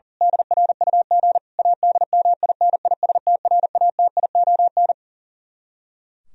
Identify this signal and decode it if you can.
Morse code, heard as a keyed tone, about 37 wpm, 700 Hz. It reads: DRUG ADMINISTRATION
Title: Speed-Racing - Sets of 5 Words 40wpm